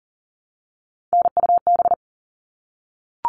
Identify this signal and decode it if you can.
Morse code, heard as a keyed tone, about 40 wpm, 700 Hz. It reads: NUB
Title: All Cut Numbers - 3-Digit 40wpm